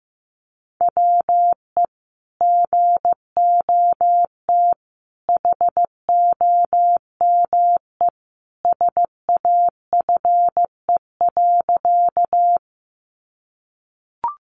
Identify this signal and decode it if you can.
Morse code, heard as a keyed tone, about 15 wpm, 700 Hz. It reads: WE GOT HOME SAFE.
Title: Sentences from Top 900 Words 15wpm